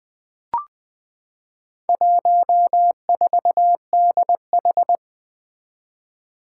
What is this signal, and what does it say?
Morse code, heard as a keyed tone, about 20 wpm, 700 Hz. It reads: 14DH
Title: Four Challenging Letter-Number Combinations 20wpm